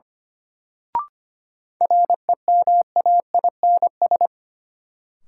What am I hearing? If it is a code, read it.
Morse code, heard as a keyed tone, about 25 wpm, 700 Hz. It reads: REMAINS
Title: Sets of 5 Words Encore - New Word Drill - 25wpm